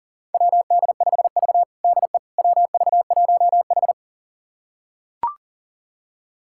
Morse code, heard as a keyed tone, about 37 wpm, 700 Hz. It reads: WD5V DE WV1H
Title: USA Callsign DE Callsign - 4 Characters - 40wpm